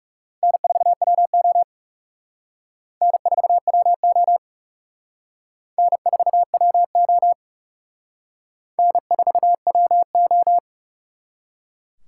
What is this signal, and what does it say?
Morse code, heard as a keyed tone, about 40 wpm, 700 Hz. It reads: N4WO N4WO N4WO N4WO